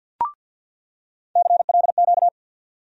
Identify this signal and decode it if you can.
Morse code, heard as a keyed tone, about 48 wpm, 700 Hz. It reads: CLQ
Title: USA Call Sign Postfixes - 3 Letters 50wpm